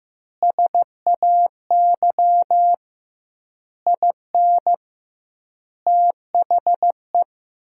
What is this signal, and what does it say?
Morse code, heard as a keyed tone, about 15 wpm, 700 Hz. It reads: SAY IN THE
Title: 2x Word-Spacing - Sentences from Top 300 Words - 15wpm